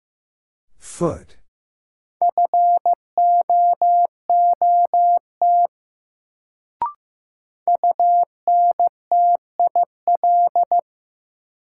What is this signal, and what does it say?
Morse code, heard as a keyed tone, about 15 wpm, 700 Hz. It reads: FOOT UNTIL
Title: Sentences from Top 500 Words - New Word Drill - 15wpm